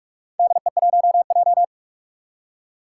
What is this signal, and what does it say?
Morse code, heard as a keyed tone, about 44 wpm, 700 Hz. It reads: DE1J
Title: Four Challenging Letter-Number Combinations 45wpm